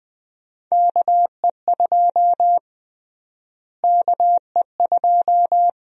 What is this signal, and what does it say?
Morse code, heard as a keyed tone, about 20 wpm, 700 Hz. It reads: KE2 KE2
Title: USA Call Sign Prefixes - Repeated 2x - 20wpm